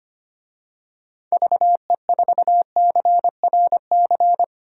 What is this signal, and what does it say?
Morse code, heard as a keyed tone, about 25 wpm, 700 Hz. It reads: VE4CRC
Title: DXCC-100 Call Signs - 6 Characters - Repeated 2x - 25wpm